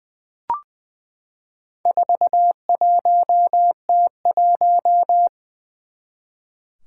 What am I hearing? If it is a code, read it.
Morse code, heard as a keyed tone, about 20 wpm, 700 Hz. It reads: 41T1
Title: Four Challenging Letter-Number Combinations 20wpm